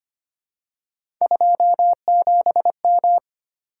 Morse code, heard as a keyed tone, about 25 wpm, 700 Hz. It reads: 27M
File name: Three Letter-Number Combinations - 25wpm